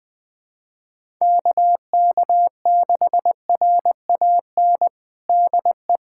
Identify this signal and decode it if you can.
Morse code, heard as a keyed tone, about 20 wpm, 700 Hz. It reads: KK6RAN DE
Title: USA Callsign DE Callsign - 6 Characters - 20wpm